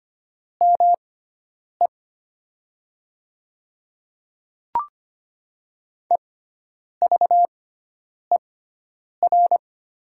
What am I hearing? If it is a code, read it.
Morse code, heard as a keyed tone, about 25 wpm, spaced as for 9 wpm, 700 Hz. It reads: ME EVER
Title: 6x Character Spacing - Top 500 Words - 25wpm